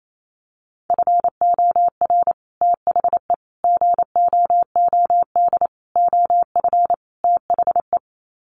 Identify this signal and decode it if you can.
Morse code, heard as a keyed tone, about 28 wpm, 700 Hz. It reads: FOR THE GOOD OF THE
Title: Sets of 5 Words Encore2 28wpm